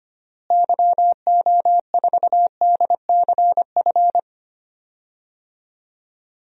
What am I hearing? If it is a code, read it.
Morse code, heard as a keyed tone, about 25 wpm, 700 Hz. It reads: YO4DCF
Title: DXCC-100 Call Signs - 6 Characters - Repeated 2x - 25wpm